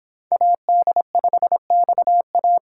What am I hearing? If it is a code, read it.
Morse code, heard as a keyed tone, about 25 wpm, 700 Hz. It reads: AD5XA